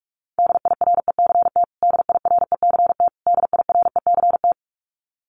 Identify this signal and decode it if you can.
Morse code, heard as a keyed tone, about 46 wpm, 700 Hz. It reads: DIRECT DIRECT DIRECT
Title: Top 500 Words Repeated 45wpm